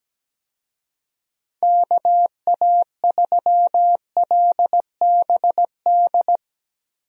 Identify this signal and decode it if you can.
Morse code, heard as a keyed tone, about 17 wpm, 700 Hz. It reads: KA3LBD
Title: USA Call Signs - 6 Characters - Repeated 2x - 17wpm